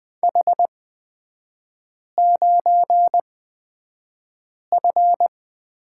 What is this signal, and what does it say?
Morse code, heard as a keyed tone, about 20 wpm, spaced as for 5 wpm, 700 Hz. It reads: H9F
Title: 224 Lesson 36e - BT - Three Characters 20wpm